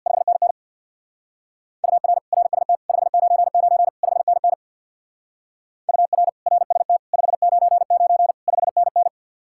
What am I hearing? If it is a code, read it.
Morse code, heard as a keyed tone, about 65 wpm, 700 Hz. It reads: 5NN UR RST 599 5NN UR RST 599 5NN